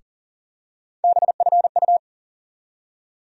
Morse code, heard as a keyed tone, about 40 wpm, 700 Hz. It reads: BFU
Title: USA Call Sign Postfixes - 3 Letters 40wpm